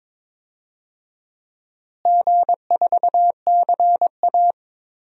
Morse code, heard as a keyed tone, about 22 wpm, 700 Hz. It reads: G4CA
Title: DXCC-100 Call Signs - 4 Characters - Repeated 2x - 22wpm